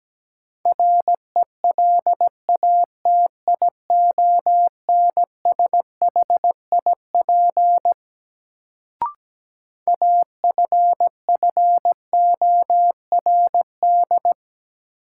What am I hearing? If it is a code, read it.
Morse code, heard as a keyed tone, about 17 wpm, 700 Hz. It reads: RELATIONSHIP AFFORD
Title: Sentences from Top 1100 Words - New Word Drill - 17wpm